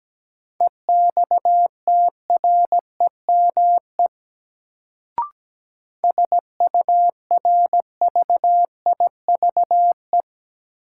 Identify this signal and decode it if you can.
Morse code, heard as a keyed tone, about 17 wpm, 700 Hz. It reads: EXTREME SURVIVE
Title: Sentences from Top 1300 Words - New Word Drill Alt - 17wpm